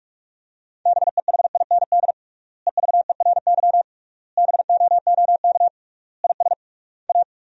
Morse code, heard as a keyed tone, about 41 wpm, 700 Hz. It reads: BEHIND EVERY BOOK IS A
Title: 3x Word-Spacing - Sentences from Top 500 Words - 45wpm